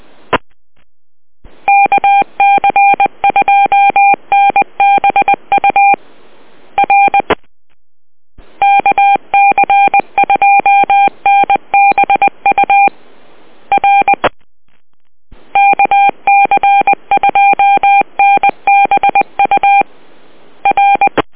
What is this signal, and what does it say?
Morse code, heard as a keyed tone, about 20 wpm, 800 Hz. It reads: KC2NBU R KC2NBU R KC2NBU R